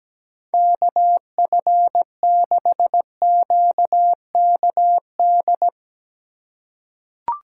Morse code, heard as a keyed tone, about 17 wpm, 700 Hz. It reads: KF6QKD
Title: USA Call Signs - 6 Characters - 2x Character Spacing - 17wpm